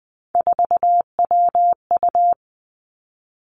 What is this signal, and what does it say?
Morse code, heard as a keyed tone, about 20 wpm, 700 Hz. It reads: 4WU
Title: Three Challenging Letter-Number Combinations 20wpm